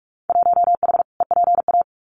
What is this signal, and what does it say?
Morse code, heard as a keyed tone, about 46 wpm, 700 Hz. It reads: 1H EPA
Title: Winter Field Day Exchanges 45wpm